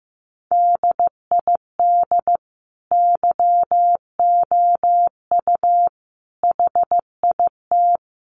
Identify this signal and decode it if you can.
Morse code, heard as a keyed tone, about 15 wpm, 700 Hz. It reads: DID YOU HIT
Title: Sentences from Top 1400 Words 15wpm